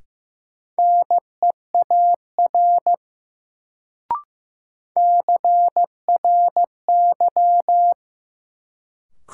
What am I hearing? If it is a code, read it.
Morse code, heard as a keyed tone, about 15 wpm, 700 Hz. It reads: NEAR CRY
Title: Sets of 5 Words Encore - New Word Drill - 15wpm